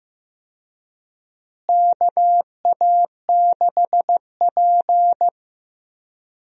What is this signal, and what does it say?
Morse code, heard as a keyed tone, about 15 wpm, 700 Hz. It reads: KA6P
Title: USA Call Signs - 4 Characters - Repeated 2x - 15wpm